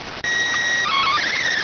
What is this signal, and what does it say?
SSTV (slow-scan television), PD 290.